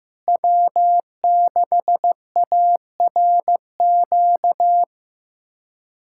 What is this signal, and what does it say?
Morse code, heard as a keyed tone, about 15 wpm, 700 Hz. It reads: W6ARQ
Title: USA Callsign DE Callsign - 5 Characters - 15wpm